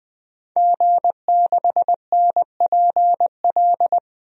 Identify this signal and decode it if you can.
Morse code, heard as a keyed tone, about 21 wpm, 700 Hz. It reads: G6NPL
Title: DXCC-100 Call Signs - 5 Characters - 20wpm